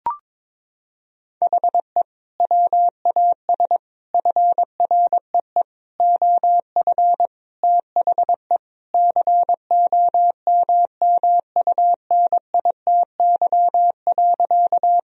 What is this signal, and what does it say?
Morse code, heard as a keyed tone, about 22 wpm, 700 Hz. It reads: HE WAS FREE OF THE COMMUNITY.
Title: Sentences from Top 1000 Words - Review Part 4 - 22wpm